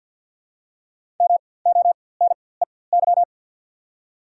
Morse code, heard as a keyed tone, about 52 wpm, spaced as for 23 wpm, 700 Hz. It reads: MONEY
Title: 4x Character Spacing - Top 500 Words - 50wpm